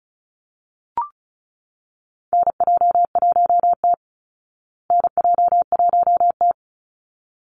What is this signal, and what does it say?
Morse code, heard as a keyed tone, about 34 wpm, 700 Hz. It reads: NJ1T NJ1T
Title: Call Signs Repeated 3x - 35wpm